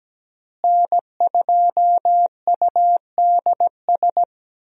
Morse code, heard as a keyed tone, about 17 wpm, 700 Hz. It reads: N2UDS
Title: USA Call Signs - 5 Characters - Repeated 2x - 17wpm